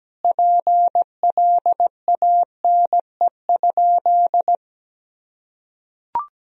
Morse code, heard as a keyed tone, about 17 wpm, 700 Hz. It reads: PLANE?
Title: Sentences from Top 1000 Words - Review Part 3 - 17wpm